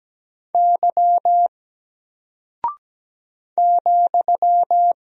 Morse code, heard as a keyed tone, about 17 wpm, 700 Hz. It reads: Y ,
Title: Single Character (Letter:Number:Punctuation:Prosign) Alt 17wpm